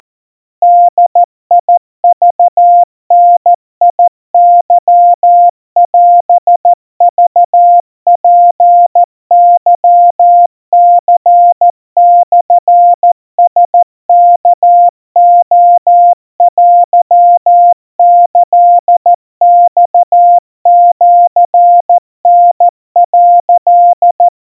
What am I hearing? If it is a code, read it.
Morse code (Japanese Wabun code), heard as a keyed tone, about 14 wpm, 700 Hz. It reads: ボクダケオクツケニモラワレテキマシタ」
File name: kawa / Morse sample / kamyu 50 cpm 700 Hz Crispy